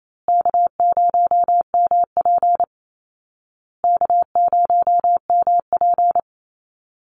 Morse code, heard as a keyed tone, about 28 wpm, 700 Hz. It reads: K0MP K0MP
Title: Easy Call Signs Repeated 3x - 28wpm